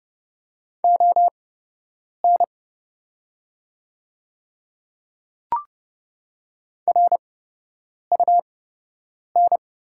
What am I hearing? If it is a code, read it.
Morse code, heard as a keyed tone, about 30 wpm, spaced as for 8 wpm, 700 Hz. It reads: ON RUN
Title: 8x Character Spacing - Top 300 Words - 30wpm